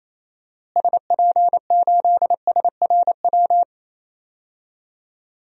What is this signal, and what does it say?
Morse code, heard as a keyed tone, about 28 wpm, 700 Hz. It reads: SP8SRW
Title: DXCC-100 Call Signs - 6 Characters - Repeated 3x - 28wpm